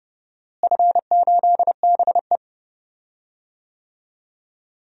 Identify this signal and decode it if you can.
Morse code, heard as a keyed tone, about 30 wpm, 700 Hz. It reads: F8BE
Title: DXCC-100 Call Signs - 4 Characters - Repeated 2x - 30wpm